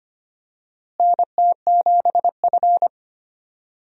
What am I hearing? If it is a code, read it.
Morse code, heard as a keyed tone, about 25 wpm, 700 Hz. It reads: NT7F
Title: USA Call Signs - 4 Characters - Repeated 3x - 25wpm